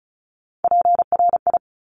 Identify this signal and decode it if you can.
Morse code, heard as a keyed tone, about 36 wpm, 700 Hz. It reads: PRI
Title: Top Three Letter Combinations in Words 35wpm